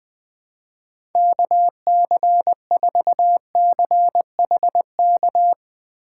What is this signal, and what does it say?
Morse code, heard as a keyed tone, about 21 wpm, 700 Hz. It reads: KC4CHK
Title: USA Call Signs - 6 Characters - 3x Character Spacing - 20wpm